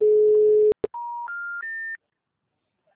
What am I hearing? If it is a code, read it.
Special information tone.